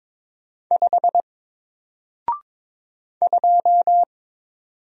Morse code, heard as a keyed tone, about 22 wpm, 700 Hz. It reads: 5 2
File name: Single-Letter-Number-22wpm